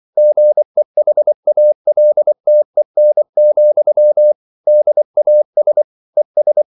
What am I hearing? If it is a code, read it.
Morse code, heard as a keyed tone, about 24 wpm, 600 Hz. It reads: GEHALTEN, DAS ES